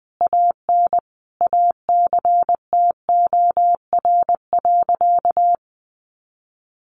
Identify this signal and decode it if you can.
Morse code, heard as a keyed tone, about 20 wpm, 700 Hz. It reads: AN ACTOR.
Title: Sentences from Top 1500 Words 20wpm